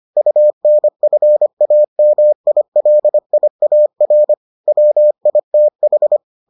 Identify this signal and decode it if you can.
Morse code, heard as a keyed tone, about 25 wpm, 600 Hz. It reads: UNFAMILIAR WITH